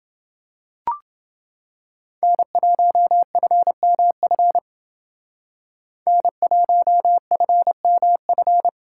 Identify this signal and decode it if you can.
Morse code, heard as a keyed tone, about 29 wpm, 700 Hz. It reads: N1FMF N1FMF